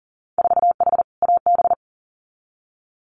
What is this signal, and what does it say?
Morse code, heard as a keyed tone, about 40 wpm, 700 Hz. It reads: 4H AB
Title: Winter Field Day Exchanges Encore 40wpm